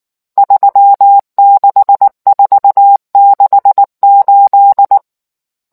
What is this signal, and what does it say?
Morse code, heard as a keyed tone, about 19 wpm, 800 Hz. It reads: 36468